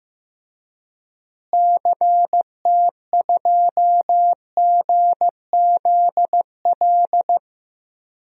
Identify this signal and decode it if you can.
Morse code, heard as a keyed tone, about 15 wpm, 700 Hz. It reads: CT2GZL
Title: DXCC-100 Call Signs - 6 Characters - Repeated 2x - 15wpm